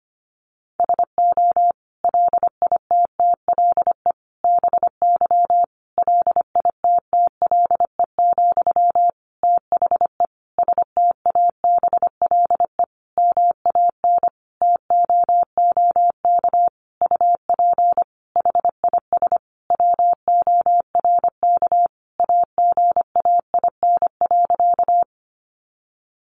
Morse code, heard as a keyed tone, about 25 wpm, 700 Hz. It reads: SO LITTLE BY LITTLE, THE STABLE MAN TOOK UP HIS WORK AGAIN.
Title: Sentences from Top 1500 Words 25wpm